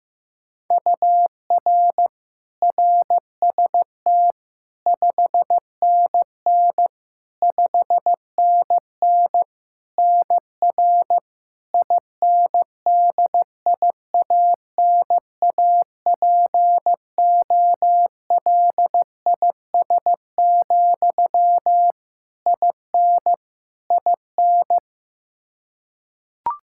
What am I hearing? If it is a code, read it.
Morse code, heard as a keyed tone, about 15 wpm, 700 Hz. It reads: UR RST 5NN 5NN NR INDIANAPOLIS, IN IN
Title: K1USN SST QSO Practice 15wpm